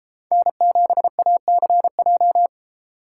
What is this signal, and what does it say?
Morse code, heard as a keyed tone, about 33 wpm, 700 Hz. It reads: N7ACJ